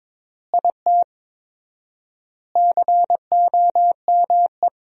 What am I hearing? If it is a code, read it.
Morse code, heard as a keyed tone, about 21 wpm, 700 Hz. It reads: IT COME
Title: 4x Word-Spacing - Sentences from Top 100 Words - 22wpm